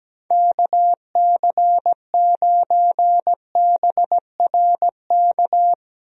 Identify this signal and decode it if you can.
Morse code, heard as a keyed tone, about 17 wpm, 700 Hz. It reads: KC9BRK
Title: USA Call Signs - 6 Characters - 17wpm